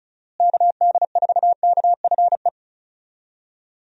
Morse code, heard as a keyed tone, about 34 wpm, 700 Hz. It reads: KD4KFE
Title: USA Call Signs - 6 Characters - Repeated 3x - 35wpm